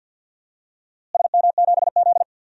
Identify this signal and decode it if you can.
Morse code, heard as a keyed tone, about 52 wpm, 700 Hz. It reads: SM7Z
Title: DXCC-100 Call Signs - 4 Characters - Repeated 2x - 50wpm